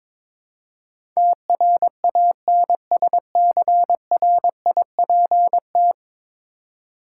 Morse code, heard as a keyed tone, about 22 wpm, 700 Hz. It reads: TRANSCRIPT
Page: morse-code-ninja-practice-22wpm